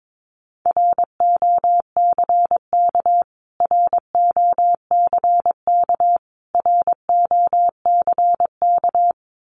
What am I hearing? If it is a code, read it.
Morse code, heard as a keyed tone, about 22 wpm, 700 Hz. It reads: ROCK ROCK ROCK
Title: Top 500 Words Repeated 22wpm